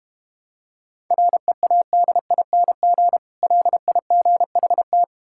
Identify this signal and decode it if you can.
Morse code, heard as a keyed tone, about 31 wpm, 700 Hz. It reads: READING LIGHT